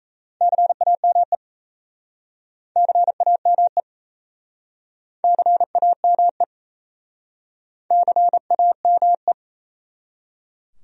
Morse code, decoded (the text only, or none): CAME CAME CAME CAME